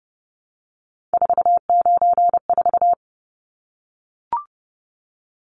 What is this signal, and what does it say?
Morse code, heard as a keyed tone, about 30 wpm, 700 Hz. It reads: 494
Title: Numbers - 3-Digit 30wpm